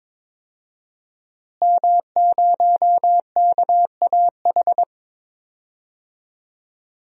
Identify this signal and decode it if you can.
Morse code, heard as a keyed tone, about 22 wpm, 700 Hz. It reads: M0KAH
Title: DXCC-100 Call Signs - 5 Characters - Repeated 3x - 22wpm